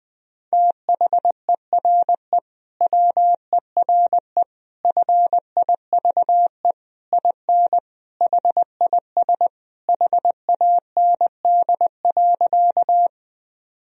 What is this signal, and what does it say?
Morse code, heard as a keyed tone, about 20 wpm, 700 Hz. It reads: THERE WERE FIVE IN HIS HAND.
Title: Sentences from Top 1000 Words - Review Part 2 - 20wpm